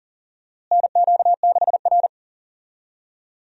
Morse code, heard as a keyed tone, about 38 wpm, 700 Hz. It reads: NQ6R
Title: USA Call Signs - 4 Characters - Repeated 2x - 40wpm